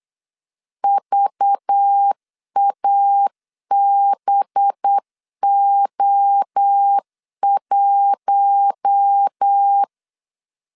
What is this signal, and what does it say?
Morse code, heard as a keyed tone, about 8 wpm, 800 Hz. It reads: VABO1
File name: CW / DET / 1 6 1500C S B 8wpm 5142